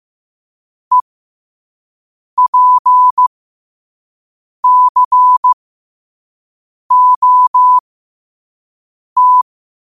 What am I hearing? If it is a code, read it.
Morse code, heard as a keyed tone, about 15 wpm, spaced as for 5 wpm, 1000 Hz. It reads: EPCOT
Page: qotd-at-5-wpm